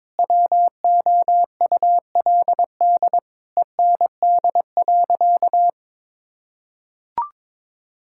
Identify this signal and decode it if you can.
Morse code, heard as a keyed tone, about 22 wpm, 700 Hz. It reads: WOULD END.
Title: Sentences from Top 1600 Words 22wpm